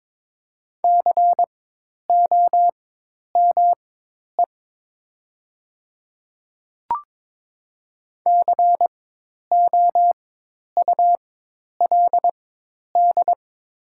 Morse code, heard as a keyed tone, about 22 wpm, spaced as for 10 wpm, 700 Hz. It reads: COME COULD
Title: 4x Character Spacing - Top 200 Words - 22wpm